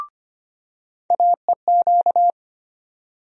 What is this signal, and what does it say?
Morse code, heard as a keyed tone, about 25 wpm, 700 Hz. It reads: AEQ
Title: USA Call Sign Postfixes - 3 Letters 25wpm